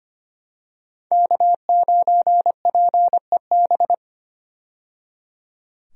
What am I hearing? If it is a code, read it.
Morse code, heard as a keyed tone, about 25 wpm, 700 Hz. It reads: K9PEB
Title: USA Call Signs - 5 Characters - 25wpm